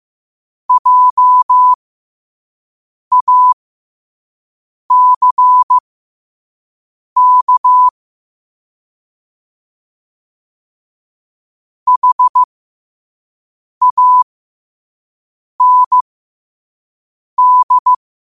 Morse code, heard as a keyed tone, about 15 wpm, spaced as for 5 wpm, 1000 Hz. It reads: JACK HAND
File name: qotdfiles / 5wpm